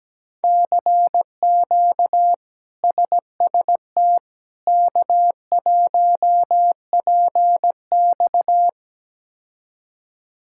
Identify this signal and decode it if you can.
Morse code, heard as a keyed tone, about 17 wpm, 700 Hz. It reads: CQ SST K1PX
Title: K1USN SST QSO Practice - 17wpm